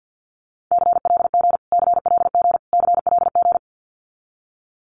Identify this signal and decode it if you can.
Morse code, heard as a keyed tone, about 50 wpm, 700 Hz. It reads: CLG CLG CLG
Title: Abbreviations Repeated 3x - 50wpm